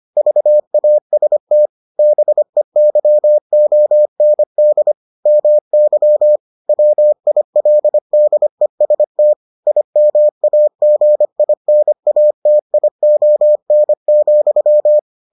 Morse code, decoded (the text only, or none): VAST BEYOND MY WILDEST IMAGINATION,